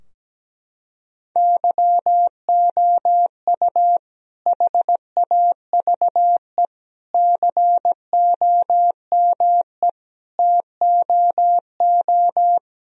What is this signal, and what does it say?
Morse code, heard as a keyed tone, about 17 wpm, 700 Hz. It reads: YOU HAVE COME TOO
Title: Sentences from Top 1000 Words - Review Part 2 - 17wpm